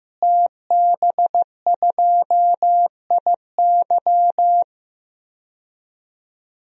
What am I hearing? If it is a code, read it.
Morse code, heard as a keyed tone, about 15 wpm, 700 Hz. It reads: TB2IY
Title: DXCC-100 Call Signs - 5 Characters - Repeated 2x - 15wpm